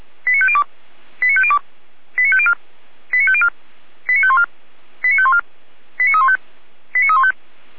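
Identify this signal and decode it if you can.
Five-tone selective call (ZVEI), heard as quick runs of five tones, tones of about 70 ms. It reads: 89502 89502 89504 89504 89514 89514 89315 89315
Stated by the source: ZVEI five-tone call 89230 received by SDR.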